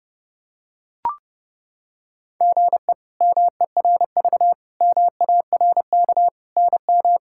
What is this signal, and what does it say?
Morse code, heard as a keyed tone, about 32 wpm, 700 Hz. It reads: GE MERV MARK NM
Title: K1USN SST QSO Practice - 30wpm